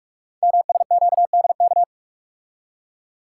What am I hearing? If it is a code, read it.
Morse code, heard as a keyed tone, about 44 wpm, 700 Hz. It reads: MSQDK
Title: Code Groups - 5 Letters - 45wpm